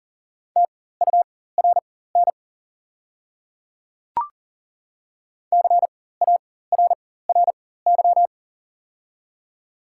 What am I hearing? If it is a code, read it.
Morse code, heard as a keyed tone, about 42 wpm, spaced as for 19 wpm, 690 Hz. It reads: TURN CARRY
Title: 4x Character Spacing - Top 300 Words - 40wpm